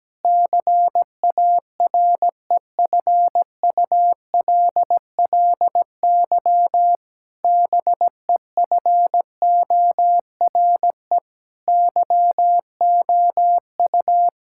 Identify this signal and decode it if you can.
Morse code, heard as a keyed tone, about 17 wpm, 700 Hz. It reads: CAREFULLY BEFORE YOU
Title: Sentences from Top 1600 Words 17wpm